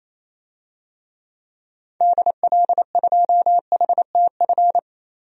Morse code, heard as a keyed tone, about 28 wpm, 700 Hz. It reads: DL2HTF
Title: DXCC-100 Call Signs - 6 Characters - 28wpm